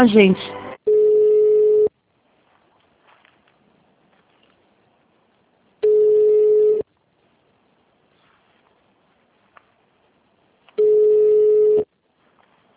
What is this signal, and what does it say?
Ringback tone (Europe, 425 Hz), with speech.